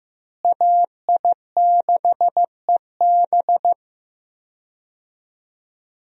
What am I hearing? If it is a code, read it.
Morse code, heard as a keyed tone, about 15 wpm, 700 Hz. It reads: AI6EB
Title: USA Call Signs - 5 Characters - Repeated 3x - 15wpm